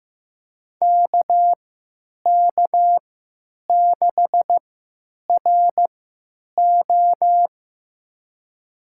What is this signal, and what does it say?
Morse code, heard as a keyed tone, about 15 wpm, spaced as for 9 wpm, 700 Hz. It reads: KK6RO
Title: USA Call Signs - 5 Characters - 3x Character Spacing - 15wpm